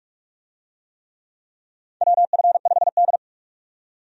Morse code, heard as a keyed tone, about 41 wpm, 700 Hz. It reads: WF5D
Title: USA Call Signs - 4 Characters - 45wpm